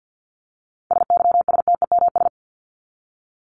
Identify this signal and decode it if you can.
Morse code, heard as a keyed tone, about 50 wpm, 700 Hz. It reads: SYSTEMS
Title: ES:S Suffix - Top 100 Words with Base Words 50wpm